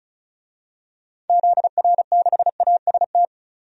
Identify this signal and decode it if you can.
Morse code, heard as a keyed tone, about 36 wpm, 700 Hz. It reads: ZR6AST
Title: DXCC-100 Call Signs - 6 Characters - 35wpm